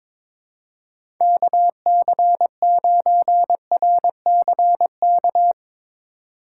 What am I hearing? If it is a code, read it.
Morse code, heard as a keyed tone, about 22 wpm, 700 Hz. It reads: KC9RCK
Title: USA Call Signs - 6 Characters - 3x Character Spacing - 22wpm